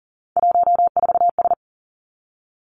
Morse code, heard as a keyed tone, about 42 wpm, 700 Hz. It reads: 14S
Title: Three Challenging Letter-Number Combinations 40wpm